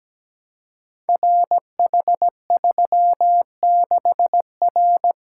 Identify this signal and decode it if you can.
Morse code, heard as a keyed tone, about 17 wpm, 700 Hz. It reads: RH36R
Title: Code Groups - 5 Letter:Numbers - 17wpm